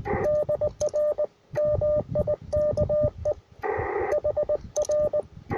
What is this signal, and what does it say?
Morse code, heard as a keyed tone, about 20 wpm, 600 Hz, with radio noise. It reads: DR MIKE HR
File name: attachments / R7AO UR4MCK